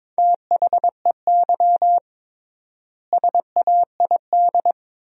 Morse code, heard as a keyed tone, about 21 wpm, 700 Hz. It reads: THEY SAID
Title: 3x Word-Spacing - Sentences from Top 100 Words - 22wpm